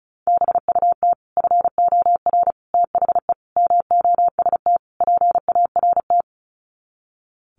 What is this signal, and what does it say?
Morse code, heard as a keyed tone, about 34 wpm, 700 Hz. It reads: BUT FOR THE MOST PART
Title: Sets of 5 Words Encore 35wpm